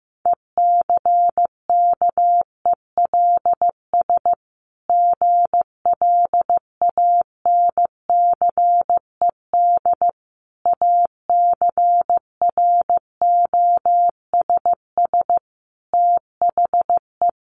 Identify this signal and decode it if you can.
Morse code, heard as a keyed tone, about 15 wpm, 700 Hz. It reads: ECKELS GLANCED ACROSS THE
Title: A Sound Of Thunder 15wpm